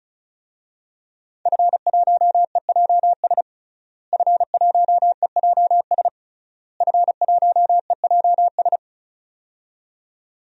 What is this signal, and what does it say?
Morse code, heard as a keyed tone, about 36 wpm, 700 Hz. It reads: F1EJS F1EJS F1EJS
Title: DXCC-100 Call Signs - 5 Characters - Repeated 3x - 35wpm